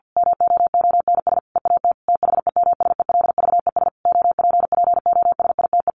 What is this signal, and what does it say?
Morse code, heard as a keyed tone, about 50 wpm, 700 Hz. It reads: MOONS EAT THEMSELVES OPPOSITE
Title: A Sound Of Thunder 50wpm